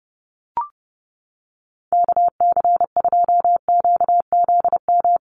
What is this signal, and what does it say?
Morse code, heard as a keyed tone, about 30 wpm, 700 Hz. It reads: KC2QZM
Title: USA Call Signs Repeated 2x 30wpm